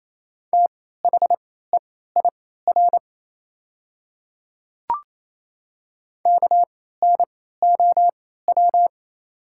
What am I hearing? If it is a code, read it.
Morse code, heard as a keyed tone, about 28 wpm, spaced as for 16 wpm, 700 Hz. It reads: THEIR KNOW
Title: 3x Character Spacing - Top 100 Words - 28wpm